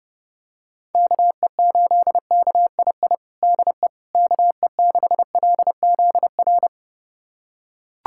Morse code, heard as a keyed tone, about 30 wpm, 700 Hz. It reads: KE8KII DE KE6LZR
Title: USA Callsign DE Callsign - 6 Characters - 30wpm